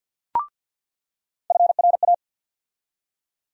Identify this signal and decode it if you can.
Morse code, heard as a keyed tone, about 48 wpm, 700 Hz. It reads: FRA
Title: USA Call Sign Postfixes - 3 Letters 50wpm